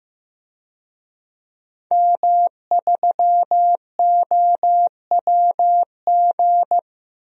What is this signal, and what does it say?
Morse code, heard as a keyed tone, about 15 wpm, 700 Hz. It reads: M3OWG
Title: DXCC-100 Call Signs - 5 Characters - 15wpm